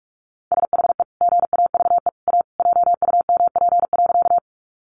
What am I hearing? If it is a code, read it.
Morse code, heard as a keyed tone, about 44 wpm, 700 Hz. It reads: SHE GAVE A JUMP.
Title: Sentences from Top 1300 Words 45wpm